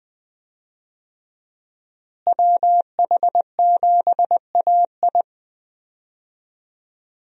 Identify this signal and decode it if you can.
Morse code, heard as a keyed tone, about 20 wpm, 700 Hz. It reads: WH7AI
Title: USA Call Signs - 5 Characters - Repeated 2x - 20wpm